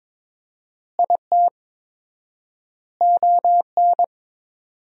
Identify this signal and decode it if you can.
Morse code, heard as a keyed tone, about 22 wpm, 700 Hz. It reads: IT ON
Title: 4x Word-Spacing - Sentences from Top 400 Words - 22wpm